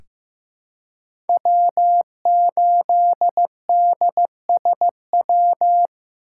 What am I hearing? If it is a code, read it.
Morse code, heard as a keyed tone, about 15 wpm, 700 Hz. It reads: W8DSW
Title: USA Call Signs - 5 Characters - 2x Character Spacing - 15wpm